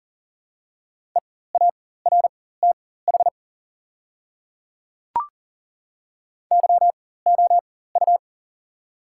Morse code, heard as a keyed tone, about 40 wpm, spaced as for 19 wpm, 690 Hz. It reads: EARTH YOU
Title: 4x Character Spacing - Top 200 Words - 40wpm